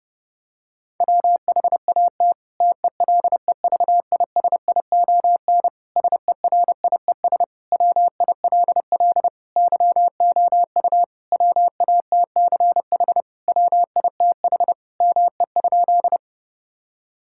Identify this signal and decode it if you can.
Morse code, heard as a keyed tone, about 32 wpm, 700 Hz. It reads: WHAT TELEVISION SERIES WILL YOU WATCH WITH ME?
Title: Sentences from Top 1000 Words - Review Part 5 - 30wpm